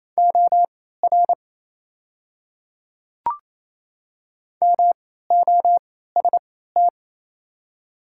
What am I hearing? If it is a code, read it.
Morse code, heard as a keyed tone, about 28 wpm, spaced as for 16 wpm, 690 Hz. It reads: OR MOST
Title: 3x Character Spacing - Top 100 Words - 28wpm